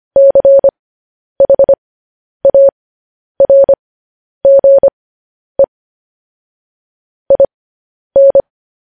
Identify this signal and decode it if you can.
Morse code, heard as a keyed tone, about 25 wpm, spaced as for 10 wpm, 550 Hz. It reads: CHARGE IN
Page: news-headlines-in-morse-code-at-10-wpm